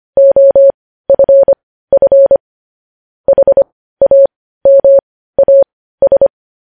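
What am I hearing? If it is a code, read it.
Morse code, heard as a keyed tone, about 25 wpm, spaced as for 15 wpm, 550 Hz. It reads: OFF HAMAS
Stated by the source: Morse news headlines sent at 15 WPM, beginning with vvv.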